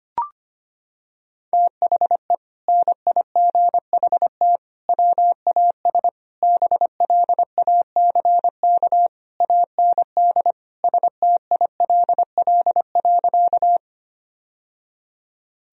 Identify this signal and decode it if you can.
Morse code, heard as a keyed tone, about 25 wpm, 700 Hz. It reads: THE NIGHT WAS BLACK AND STILL.
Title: Sentences from Top 1000 Words - Review Part 2 - 25wpm